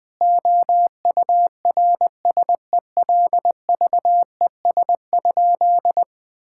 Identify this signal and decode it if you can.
Morse code, heard as a keyed tone, about 20 wpm, 700 Hz. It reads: OURSELVES?
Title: Sentences from Top 1600 Words 20wpm